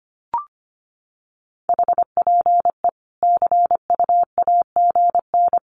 Morse code, heard as a keyed tone, about 25 wpm, 700 Hz. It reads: HPE CU AGN
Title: Common QSO Phrases 25wpm